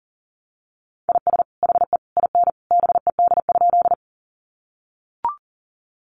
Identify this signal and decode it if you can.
Morse code, heard as a keyed tone, about 40 wpm, 700 Hz. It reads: IS HE IN BED?
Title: Sentences from Top 500 Words 40wpm